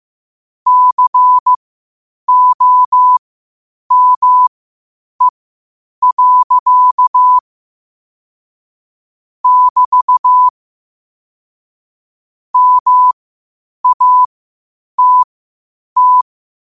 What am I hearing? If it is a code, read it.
Morse code, heard as a keyed tone, about 15 wpm, spaced as for 8 wpm, 1000 Hz. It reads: COME. <BT> MATT
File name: qotdfiles / 8wpm